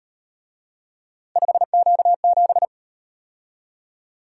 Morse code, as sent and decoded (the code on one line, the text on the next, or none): ..... --.- --...
5Q7